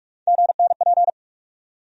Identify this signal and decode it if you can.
Morse code, heard as a keyed tone, about 46 wpm, 700 Hz. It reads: GNP
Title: USA Call Sign Postfixes - 3 Letters 45wpm